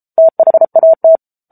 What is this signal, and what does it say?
Morse code, heard as a keyed tone, about 34 wpm, 650 Hz. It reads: THAT
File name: Book Two / 07-Chapter 7 - 33 WPM